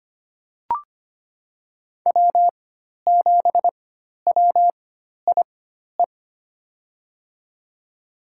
Morse code, heard as a keyed tone, about 25 wpm, spaced as for 12 wpm, 700 Hz. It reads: W7WIE
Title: USA Call Signs - 4x Character Spacing - 25wpm